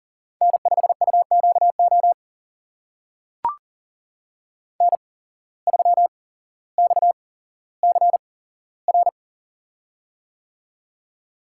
Morse code, heard as a keyed tone, about 37 wpm, spaced as for 17 wpm, 700 Hz. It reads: N5UQO N3XCR
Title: USA Call Signs - 5 Characters - 8x Character Spacing - 40wpm